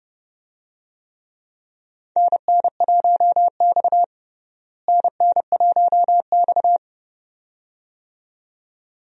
Morse code, heard as a keyed tone, about 30 wpm, 700 Hz. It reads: NN1X NN1X
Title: USA Call Signs - 4 Characters - Repeated 2x - 30wpm